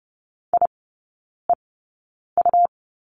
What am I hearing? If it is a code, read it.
Morse code, heard as a keyed tone, about 30 wpm, spaced as for 9 wpm, 700 Hz. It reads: IEU
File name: Lesson 17e - Letter u - Three Characters 30wpm